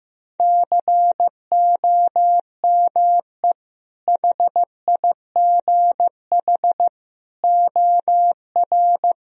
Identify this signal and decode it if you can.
Morse code, heard as a keyed tone, about 15 wpm, 700 Hz. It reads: COME HIGH OR
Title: Sentences from Top 1000 Words - Review Part 1 - 15wpm